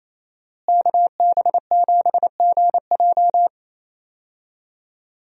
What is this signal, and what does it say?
Morse code, heard as a keyed tone, about 27 wpm, 700 Hz. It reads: KB7GJ
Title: USA Call Signs - 5 Characters - Repeated 3x - 28wpm